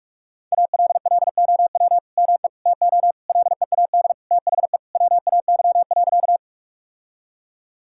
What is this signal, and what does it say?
Morse code, heard as a keyed tone, about 46 wpm, 700 Hz. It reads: ALLOW ME TO LEAD THE WAY.
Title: Sentences from Top 1000 Words - Review Part 5 - 45wpm